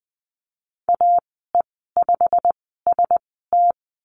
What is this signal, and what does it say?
Morse code, heard as a keyed tone, about 20 wpm, spaced as for 15 wpm, 700 Hz. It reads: AE5ST
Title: 088 Lesson 15g - Number 5 - Call Signs 20wpm